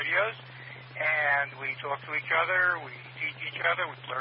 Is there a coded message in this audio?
Yes: Emergency Alert System (SAME) tones from 0.9 to 3.8 s, heard as harsh buzzing bursts, with speech.